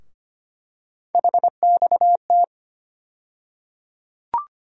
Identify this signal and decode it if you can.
Morse code, heard as a keyed tone, about 25 wpm, 700 Hz. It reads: HXT
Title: USA Call Sign Postfixes - 3 Letters 25wpm